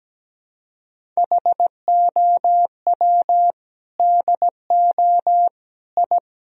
Morse code, heard as a keyed tone, about 17 wpm, 700 Hz. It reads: HOW DO I
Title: Sentences from Top 1000 Words - Review Part 1 - 17wpm